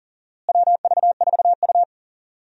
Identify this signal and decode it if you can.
Morse code, heard as a keyed tone, about 40 wpm, 700 Hz. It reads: WV4U
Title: USA Call Signs - 4 Characters - Repeated 2x - 40wpm